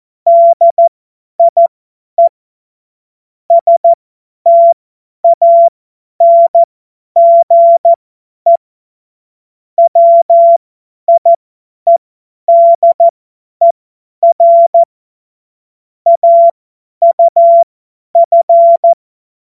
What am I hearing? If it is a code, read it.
Morse code, heard as a keyed tone, about 14 wpm, spaced as for 10 wpm, 680 Hz. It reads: DIE STANGE WIEDER AUF
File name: Morsekurs / Witze / Witz 21. Juni 2022 50 F70